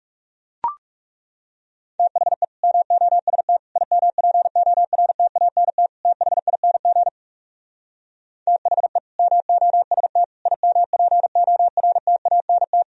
Morse code, heard as a keyed tone, about 38 wpm, 700 Hz. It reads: THE MOST IMPORTANT THING THE MOST IMPORTANT
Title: Speed-Racing - Sets of 4 Words 30wpm